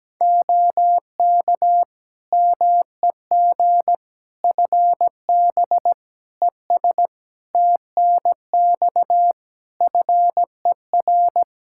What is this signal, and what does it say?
Morse code, heard as a keyed tone, about 17 wpm, 700 Hz. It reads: OK MEG FB ES TNX FER
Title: Practice QSO - Exchange 2 - Info - 17wpm